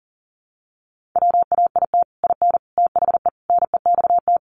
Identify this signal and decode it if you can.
Morse code, heard as a keyed tone, about 40 wpm, 700 Hz. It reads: WAIT IN THE NEXT
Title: Sentences from Top 1500 Words 40wpm